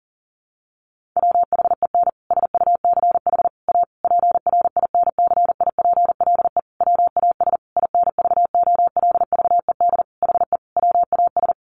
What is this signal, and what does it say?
Morse code, heard as a keyed tone, about 40 wpm, 700 Hz. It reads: WHEN SUCH A PRINCIPLE WAS INVOLVED HE WAS
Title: Sentences from Top 1400 Words 40wpm